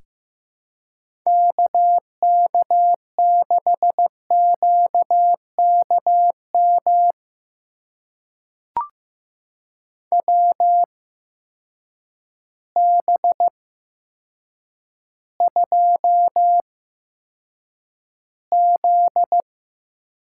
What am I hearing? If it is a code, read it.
Morse code, heard as a keyed tone, about 15 wpm, spaced as for 8 wpm, 700 Hz. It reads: KK6QKM WB2Z
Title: USA Call Signs - 8x Character Spacing - 15wpm